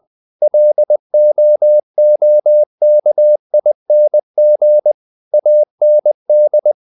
Morse code, heard as a keyed tone, about 20 wpm, 600 Hz. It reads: LOOKING AND
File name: the old man and the sea part 2 20wpm